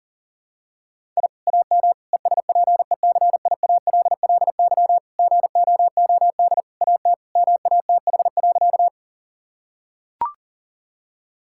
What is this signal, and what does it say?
Morse code, heard as a keyed tone, about 37 wpm, 700 Hz. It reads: I AM ESPECIALLY GOOD AT MATH.
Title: Sentences from Top 1000 Words - Review Part 6 - 40wpm